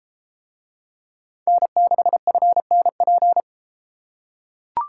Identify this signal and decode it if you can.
Morse code, heard as a keyed tone, about 31 wpm, 700 Hz. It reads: N6FNP